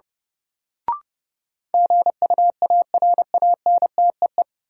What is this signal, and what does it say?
Morse code, heard as a keyed tone, about 30 wpm, 700 Hz. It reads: GUARANTEE
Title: Sentences from Top 1600 Words - New Word Drill - 30wpm